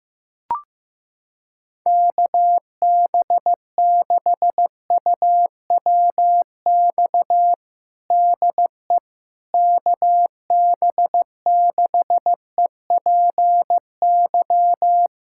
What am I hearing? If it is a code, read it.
Morse code, heard as a keyed tone, about 15 wpm, 700 Hz. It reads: KB6UWX DE KB6EPY
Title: USA Callsign DE Callsign 15wpm